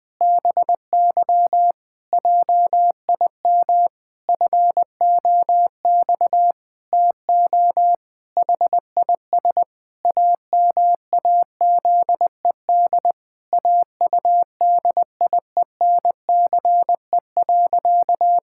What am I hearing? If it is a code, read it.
Morse code, heard as a keyed tone, about 21 wpm, 700 Hz. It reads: BY JIM FOX TO HIS AMAZED AUDIENCE.
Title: Pangram Challenge v2 - 20wpm